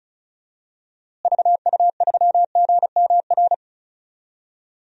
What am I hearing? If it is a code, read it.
Morse code, heard as a keyed tone, about 36 wpm, 700 Hz. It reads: VU3GMR
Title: DXCC-100 Call Signs - 6 Characters - Repeated 2x - 35wpm